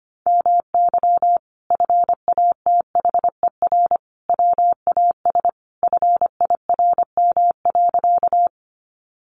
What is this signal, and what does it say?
Morse code, heard as a keyed tone, about 25 wpm, 700 Hz. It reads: MY FATHER WAS FIRM.
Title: Sentences from Top 1400 Words 25wpm